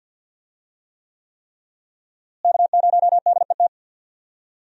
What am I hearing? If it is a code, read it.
Morse code, heard as a keyed tone, about 47 wpm, 700 Hz. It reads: K0DET
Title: USA Call Signs - 5 Characters - Repeated 3x - 50wpm